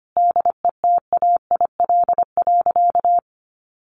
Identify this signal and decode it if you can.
Morse code, heard as a keyed tone, about 25 wpm, 700 Hz. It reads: DETAIL.
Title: Sentences from Top 1500 Words 25wpm